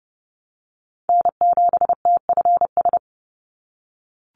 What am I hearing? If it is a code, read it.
Morse code, heard as a keyed tone, about 30 wpm, 700 Hz. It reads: N7TFS
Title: USA Call Signs 30wpm